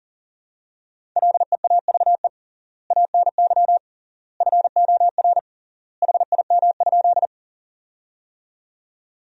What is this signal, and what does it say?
Morse code, heard as a keyed tone, about 40 wpm, 700 Hz. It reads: LEAVE ANY FOR HIM?
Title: 3x Word-Spacing - Sentences from Top 400 Words - 40wpm